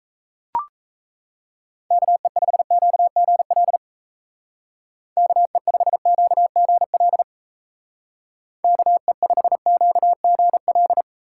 Morse code, heard as a keyed tone, about 36 wpm, 700 Hz. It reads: KE5QGL KE5QGL KE5QGL